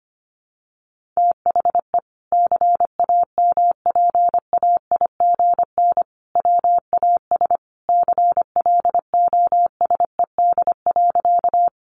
Morse code, heard as a keyed tone, about 25 wpm, 700 Hz. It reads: THE CAMPAIGN WAS CLOSED.
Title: Sentences from Top 1400 Words 25wpm